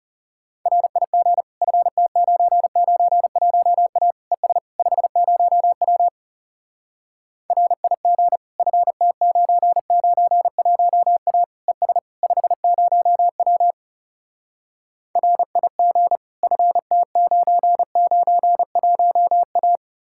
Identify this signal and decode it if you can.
Morse code, heard as a keyed tone, about 40 wpm, 700 Hz. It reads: RIG FT991A ES 50W RIG FT991A ES 50W RIG FT991A